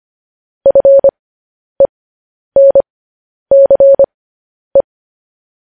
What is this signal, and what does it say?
Morse code, heard as a keyed tone, about 25 wpm, spaced as for 10 wpm, 550 Hz. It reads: FENCE
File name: podcast / News headlines-1x-25x10